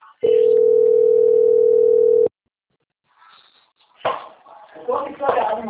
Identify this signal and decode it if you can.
Ringback tone (North America), 440 + 480 Hz, with speech.
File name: out- - - - - .3371